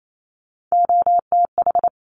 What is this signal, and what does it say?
Morse code, heard as a keyed tone, about 28 wpm, 700 Hz. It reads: OTH
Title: Top Three Letter Combinations in Words 28wpm